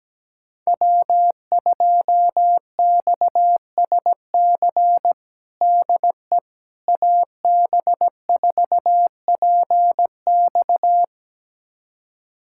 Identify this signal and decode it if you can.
Morse code, heard as a keyed tone, about 17 wpm, 700 Hz. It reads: W2XSC DE AB4PX
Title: USA Callsign DE Callsign - 5 Characters - 17wpm